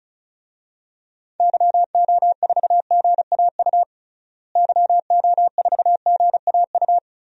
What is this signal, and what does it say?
Morse code, heard as a keyed tone, about 34 wpm, 700 Hz. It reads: YO4GAU YO4GAU
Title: DXCC-100 Call Signs - 6 Characters - Repeated 3x - 35wpm